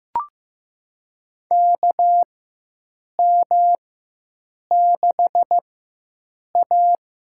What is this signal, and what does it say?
Morse code, heard as a keyed tone, about 15 wpm, spaced as for 7 wpm, 700 Hz. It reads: KM6A
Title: USA Call Signs - 4 Characters - 4x Character Spacing - 15wpm